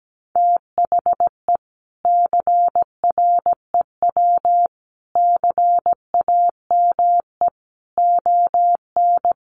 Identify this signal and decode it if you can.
Morse code, heard as a keyed tone, about 17 wpm, 700 Hz. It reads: THE CREW CAME ON
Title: Sentences from Top 1500 Words 17wpm